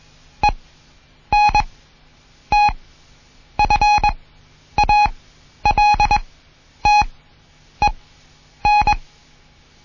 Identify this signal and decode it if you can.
Morse code, heard as a keyed tone, about 21 wpm, spaced as for 9 wpm, 830 Hz. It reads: ENTFALTEN
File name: Morsen / Kurs A 20